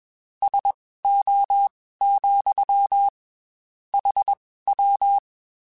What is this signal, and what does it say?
Morse code, heard as a keyed tone, about 21 wpm, spaced as for 15 wpm, 800 Hz. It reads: SO, HW
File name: QSOs / QSOs-mp3s / WD0BGP 1@20